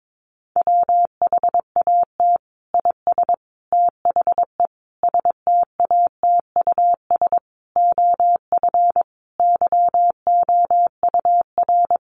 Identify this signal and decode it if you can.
Morse code, heard as a keyed tone, about 22 wpm, 700 Hz. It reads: WHAT IS THE STATUS OF YOUR
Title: Sentences from Top 1400 Words 22wpm